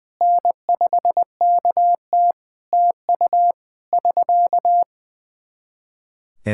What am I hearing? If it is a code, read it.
Morse code, heard as a keyed tone, about 20 wpm, 700 Hz. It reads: N5KT TU <SK>
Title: Practice QSO - Exchange 4 - Ending - 20wpm